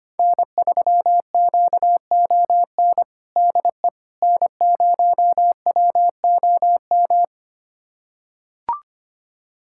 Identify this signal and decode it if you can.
Morse code, heard as a keyed tone, about 25 wpm, 700 Hz. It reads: N3QON DE N0WOM
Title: USA Callsign DE Callsign - 5 Characters - 25wpm